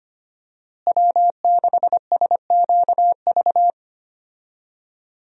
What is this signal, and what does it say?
Morse code, heard as a keyed tone, about 25 wpm, 700 Hz. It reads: W6SQV
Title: USA Call Signs - 5 Characters - Repeated 3x - 25wpm